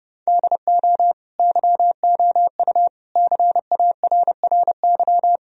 Morse code, heard as a keyed tone, about 30 wpm, 700 Hz. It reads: DO YOU CARRY
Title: Sentences from Top 1000 Words - Review Part 1 - 30wpm